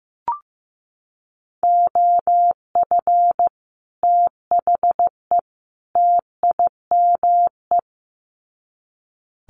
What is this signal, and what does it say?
Morse code, heard as a keyed tone, about 15 wpm, 700 Hz. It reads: OF THE TIME
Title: Sets of 3 Words 15wpm